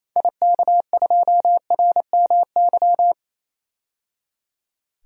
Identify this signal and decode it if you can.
Morse code, heard as a keyed tone, about 28 wpm, 700 Hz. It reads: IK2RMY
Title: DXCC-100 Call Signs - 6 Characters - 28wpm